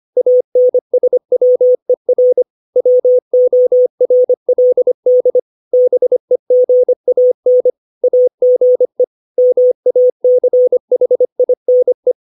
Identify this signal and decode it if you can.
Morse code, heard as a keyed tone, about 25 wpm, 500 Hz. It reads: ANSWER WORLD BEGAN AGE MACHINE
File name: 632 words / 632 Words 25WPM